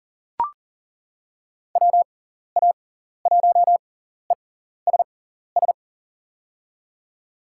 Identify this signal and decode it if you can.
Morse code, heard as a keyed tone, about 37 wpm, spaced as for 14 wpm, 700 Hz. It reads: WA1ESS
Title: USA Call Signs - 6x Character Spacing - 40wpm